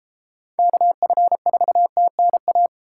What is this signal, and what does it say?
Morse code, heard as a keyed tone, about 29 wpm, 700 Hz. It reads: KF4TNA